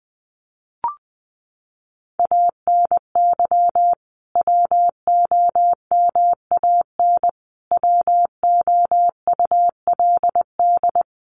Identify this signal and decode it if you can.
Morse code, heard as a keyed tone, about 20 wpm, 700 Hz. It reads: ANY WOMAN WOULD
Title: Sentences from Top 1100 Words 20wpm